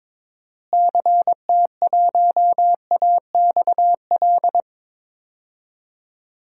Morse code, heard as a keyed tone, about 22 wpm, 700 Hz. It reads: CT1AXL
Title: DXCC-100 Call Signs - 6 Characters - Repeated 3x - 22wpm